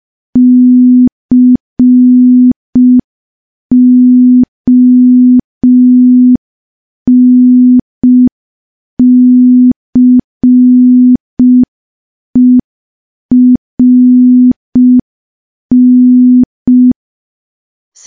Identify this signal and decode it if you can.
Morse code, heard as a keyed tone, about 5 wpm, 250 Hz. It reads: CONCERN
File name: morse / morse-training-05WPM